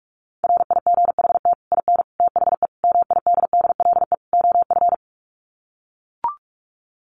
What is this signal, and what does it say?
Morse code, heard as a keyed tone, about 44 wpm, 700 Hz. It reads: RIGHT IN THE MIDDLE OF
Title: Sets of 5 Words 45wpm